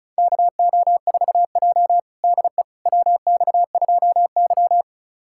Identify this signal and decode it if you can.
Morse code, heard as a keyed tone, about 34 wpm, 700 Hz. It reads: KO4J DE WX2Y
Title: USA Callsign DE Callsign - 4 Characters - 35wpm